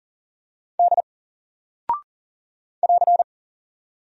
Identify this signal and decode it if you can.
Morse code, heard as a keyed tone, about 40 wpm, 700 Hz. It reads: D <AR>
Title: Single Character (Letter:Number:Punctuation:Prosign) Alt 40wpm